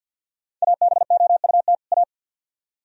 Morse code, heard as a keyed tone, about 62 wpm, 700 Hz. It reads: ABOUT A
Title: Speed-Racing - Sets of 2 Words 50wpm